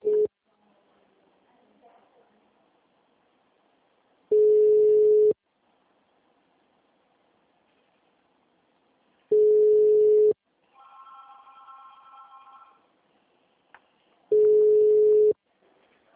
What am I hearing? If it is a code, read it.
Ringback tone (Europe, 425 Hz).